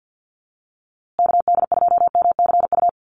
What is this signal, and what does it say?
Morse code, heard as a keyed tone, about 50 wpm, 700 Hz. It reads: KD2MCU
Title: USA Call Signs Repeated 2x 50wpm